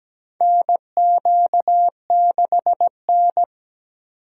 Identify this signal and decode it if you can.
Morse code, heard as a keyed tone, about 17 wpm, 700 Hz. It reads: NQ6N
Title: USA Call Signs - 3x Character Spacing - 17wpm